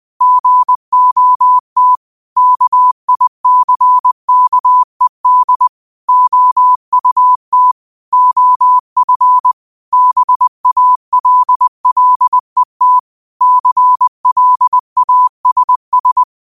Morse code, heard as a keyed tone, about 20 wpm, 1000 Hz. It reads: GOT KICKED OUT OF BALLET CLASS
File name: qotdfiles / 20wpm